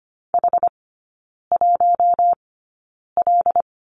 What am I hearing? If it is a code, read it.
Morse code, heard as a keyed tone, about 25 wpm, spaced as for 9 wpm, 700 Hz. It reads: H1L
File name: Lesson 15e - Number 5 - Three Characters 25wpm